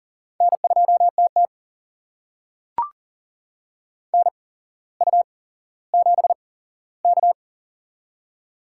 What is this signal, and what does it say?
Morse code, heard as a keyed tone, about 40 wpm, spaced as for 17 wpm, 700 Hz. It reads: N2TT NU7K
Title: USA Call Signs - 8x Character Spacing - 40wpm